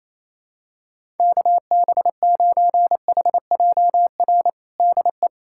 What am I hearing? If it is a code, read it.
Morse code, heard as a keyed tone, about 27 wpm, 700 Hz. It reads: KB9HJR DE
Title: USA Callsign DE Callsign - 6 Characters - 28wpm